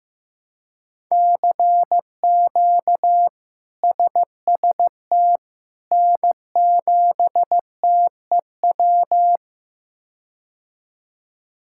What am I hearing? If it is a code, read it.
Morse code, heard as a keyed tone, about 15 wpm, 700 Hz. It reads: CQ SST N7TEW
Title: K1USN SST QSO Practice - 15wpm